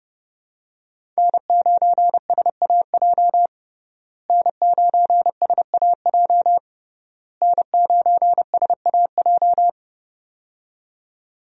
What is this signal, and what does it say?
Morse code, heard as a keyed tone, about 30 wpm, 700 Hz. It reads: N9SAJ N9SAJ N9SAJ
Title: USA Call Signs - 5 Characters - Repeated 3x - 30wpm